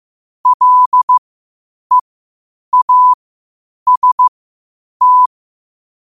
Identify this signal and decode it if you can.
Morse code, heard as a keyed tone, about 15 wpm, spaced as for 8 wpm, 1000 Hz. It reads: LEAST
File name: qotdfiles / 8wpm